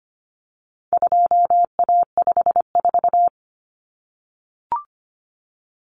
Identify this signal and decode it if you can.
Morse code, heard as a keyed tone, about 25 wpm, 700 Hz. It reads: 2A54
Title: Common Cut Numbers - 4-Digit 25wpm